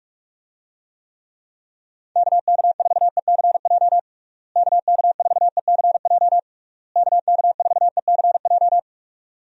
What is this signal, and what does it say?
Morse code, heard as a keyed tone, about 43 wpm, 700 Hz. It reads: KK4ECJ KK4ECJ KK4ECJ
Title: USA Call Signs - 6 Characters - Repeated 3x - 45wpm